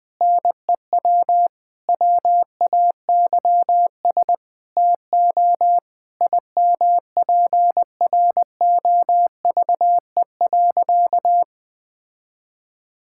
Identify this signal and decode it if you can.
Morse code, heard as a keyed tone, about 20 wpm, 700 Hz. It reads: NEW WAYS TO IMPROVE.
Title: Sentences with 6-Word Phrases - 20wpm